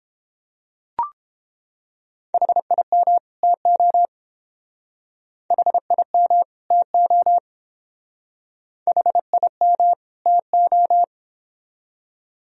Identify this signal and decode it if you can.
Morse code, heard as a keyed tone, about 28 wpm, 700 Hz. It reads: HIM TO HIM TO HIM TO